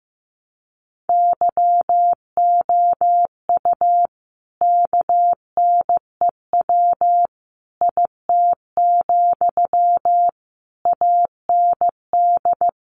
Morse code, heard as a keyed tone, about 15 wpm, 700 Hz. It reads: YOU KNEW IT, AND
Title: Sentences from Top 500 Words 15wpm